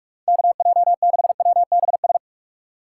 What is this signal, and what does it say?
Morse code, heard as a keyed tone, about 42 wpm, 700 Hz. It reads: KJ6WBS
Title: USA Call Signs - 4x Character Spacing - 45wpm